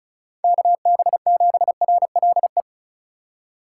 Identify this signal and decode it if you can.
Morse code, heard as a keyed tone, about 36 wpm, 700 Hz. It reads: KB7RLE
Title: USA Call Signs - 6 Characters - 3x Character Spacing - 35wpm